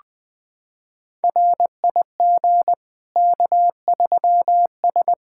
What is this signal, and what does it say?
Morse code, heard as a keyed tone, about 21 wpm, 700 Hz. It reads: RIG K3S
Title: Practice QSO - Exchange 2 - Info - 20wpm